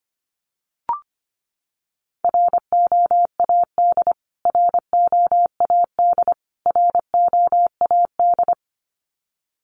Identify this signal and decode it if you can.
Morse code, heard as a keyed tone, about 25 wpm, 700 Hz. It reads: ROAD ROAD ROAD
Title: Top 500 Words Repeated 25wpm